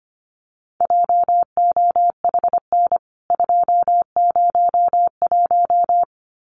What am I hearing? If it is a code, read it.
Morse code, heard as a keyed tone, about 25 wpm, 700 Hz. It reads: JOHN 201
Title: CWT Exchange 25wpm